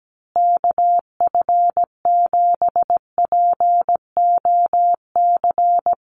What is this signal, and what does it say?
Morse code, heard as a keyed tone, about 17 wpm, 700 Hz. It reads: KF7POC
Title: USA Call Signs Repeated 2x 17wpm